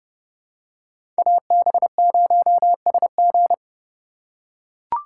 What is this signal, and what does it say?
Morse code, heard as a keyed tone, about 31 wpm, 700 Hz. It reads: AB0SG
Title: USA Call Signs - 5 Characters - 3x Character Spacing - 30wpm